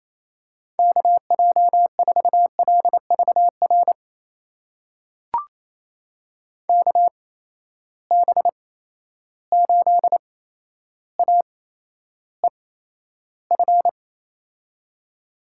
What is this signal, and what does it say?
Morse code, heard as a keyed tone, about 28 wpm, spaced as for 12 wpm, 700 Hz. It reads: KJ4LVR KB8AEF
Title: USA Call Signs - 6 Characters - 8x Character Spacing - 28wpm